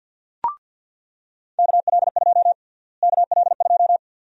Morse code, heard as a keyed tone, about 46 wpm, 700 Hz. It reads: KL2 KL2
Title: USA Call Sign Prefixes - Repeated 2x - 50wpm